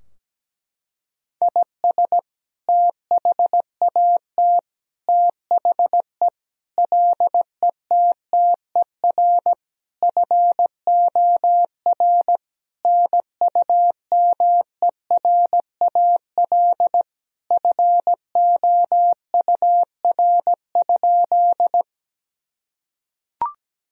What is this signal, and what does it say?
Morse code, heard as a keyed tone, about 17 wpm, 700 Hz. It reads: IS THAT THE LETTER FOR NUMERAL FOUR?
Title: Sentences from Top 1000 Words - Review Part 2 - 17wpm